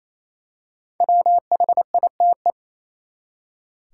Morse code, heard as a keyed tone, about 28 wpm, 700 Hz. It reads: WHITE
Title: Five Letter Words - 28wpm